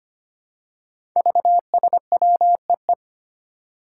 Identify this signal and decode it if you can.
Morse code, heard as a keyed tone, about 25 wpm, 700 Hz. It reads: VSWEE
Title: Five Challenging Letter-Number Combinations 25wpm